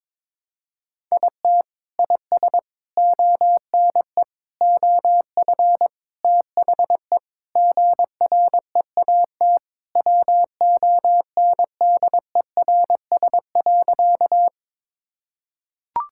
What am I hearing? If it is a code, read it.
Morse code, heard as a keyed tone, about 22 wpm, 700 Hz. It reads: IT IS ONE OF THE GREAT WONDERS.
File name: practice / 22wpm / Sentences with 4-Word Phrases - 22wpm